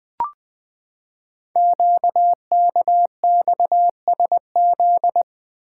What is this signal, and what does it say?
Morse code, heard as a keyed tone, about 20 wpm, 700 Hz. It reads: QKXSZ
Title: Code Groups - 5 Letters - 20wpm